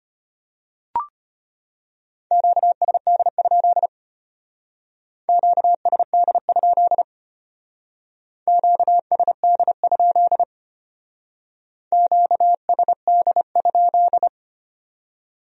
Morse code, heard as a keyed tone, about 31 wpm, 700 Hz. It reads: QSD? QSD? QSD? QSD?
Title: Speed-Racing Q-Codes - 25wpm